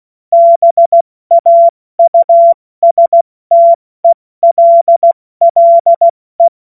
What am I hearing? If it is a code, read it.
Morse code, heard as a keyed tone, about 16 wpm, 680 Hz. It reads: BAUSTELLE
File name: Morsekurs / Witze / Witz 06. Januar 2026 70 F80